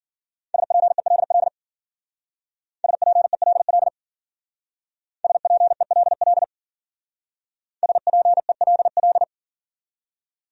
Morse code, heard as a keyed tone, about 51 wpm, 700 Hz. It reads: SPELL SPELL SPELL SPELL